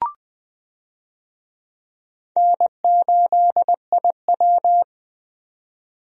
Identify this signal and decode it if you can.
Morse code, heard as a keyed tone, about 20 wpm, 700 Hz. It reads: N8IW
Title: USA Call Signs - 4 Characters - 20wpm